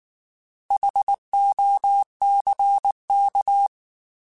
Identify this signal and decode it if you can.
Morse code, heard as a keyed tone, about 19 wpm, 800 Hz. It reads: HOCK